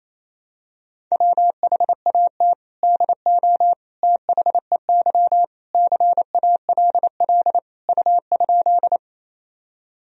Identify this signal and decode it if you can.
Morse code, heard as a keyed tone, about 28 wpm, 700 Hz. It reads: WHAT DO THEY CALL U?
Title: Sentences from Top 100 Words and Abbreviations 28wpm